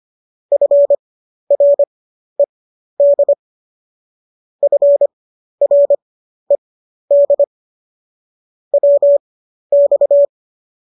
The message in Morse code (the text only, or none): FRED FRED WX